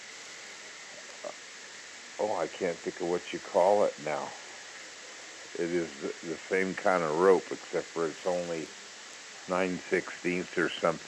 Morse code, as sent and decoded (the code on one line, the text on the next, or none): none